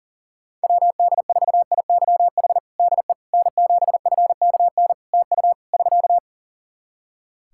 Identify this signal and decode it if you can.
Morse code, heard as a keyed tone, about 40 wpm, 700 Hz. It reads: WD4IYH DE N7FKN TU <SK>
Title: Practice QSO - Exchange 4 - Ending - 40wpm